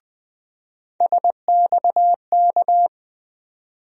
Morse code, heard as a keyed tone, about 20 wpm, 700 Hz. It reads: SXK
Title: USA Call Sign Postfixes - 3 Letters 20wpm